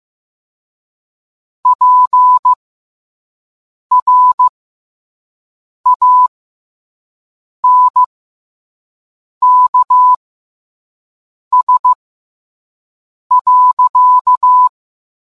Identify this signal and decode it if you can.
Morse code, heard as a keyed tone, about 15 wpm, spaced as for 5 wpm, 1000 Hz. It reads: PRANKS.